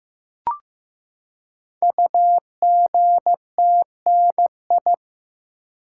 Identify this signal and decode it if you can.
Morse code, heard as a keyed tone, about 15 wpm, 700 Hz. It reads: UGTNI
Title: Code Groups - 5 Letters - 15wpm